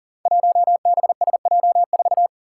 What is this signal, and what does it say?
Morse code, heard as a keyed tone, about 40 wpm, 700 Hz. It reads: 1BSJ4
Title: Five Challenging Letter-Number Combinations 40wpm